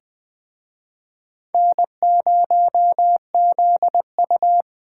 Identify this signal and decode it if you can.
Morse code, heard as a keyed tone, about 21 wpm, 700 Hz. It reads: N0ZU